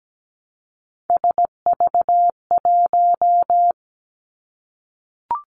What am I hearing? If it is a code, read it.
Morse code, heard as a keyed tone, about 17 wpm, 700 Hz. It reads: SV1
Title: Callsign Prefixes for Top 50 DXCC - 17wpm